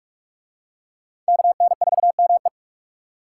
Morse code, heard as a keyed tone, about 41 wpm, 700 Hz. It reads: KN4ME